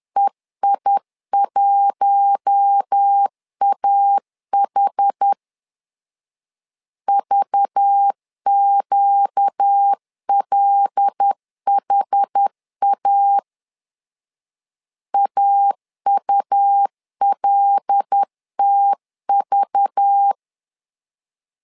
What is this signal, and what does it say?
Morse code, heard as a keyed tone, about 10 wpm, 800 Hz. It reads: EI1AH VQLHA AULTV
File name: LEZIONI / 5 LBVC / 5 LBV 10wpm 2248